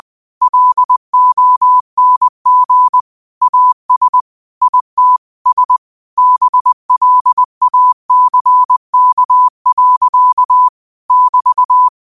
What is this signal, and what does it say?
Morse code, heard as a keyed tone, about 20 wpm, 1000 Hz. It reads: LONG AS IT S BLACK. <BT>
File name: qotdfiles / 20wpm